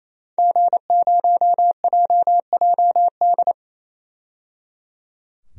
Morse code, heard as a keyed tone, about 28 wpm, 700 Hz. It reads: G0JJD
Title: DXCC-100 Call Signs - 5 Characters - 28wpm